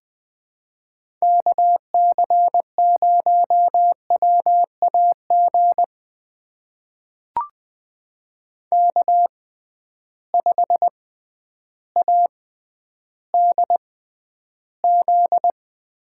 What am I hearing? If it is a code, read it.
Morse code, heard as a keyed tone, about 20 wpm, spaced as for 11 wpm, 700 Hz. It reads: KC0WAG K5ADZ
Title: USA Call Signs - 6x Character Spacing - 20wpm